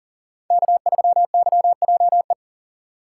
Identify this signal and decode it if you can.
Morse code, heard as a keyed tone, about 40 wpm, 700 Hz. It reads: K3YJE
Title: USA Call Signs - 5 Characters - 6x Character Spacing - 40wpm